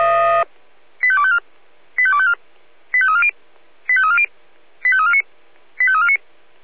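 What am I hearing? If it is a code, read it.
Five-tone selective call (ZVEI), heard as quick runs of five tones, tones of about 70 ms. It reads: 85254 85254 85340 85340 85349 85349
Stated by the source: ZVEI five-tone selective call 85352 received on SDR.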